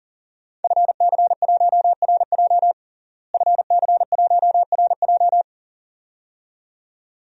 Morse code, heard as a keyed tone, about 40 wpm, 700 Hz. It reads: FC1RJ FC1RJ
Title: DXCC-100 Call Signs - 5 Characters - Repeated 2x - 40wpm